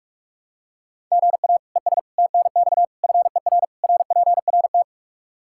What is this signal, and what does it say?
Morse code, heard as a keyed tone, about 46 wpm, 700 Hz. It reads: GA ES TNX FER RPRT
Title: Practice QSO - Exchange 1 - Beginning - 45wpm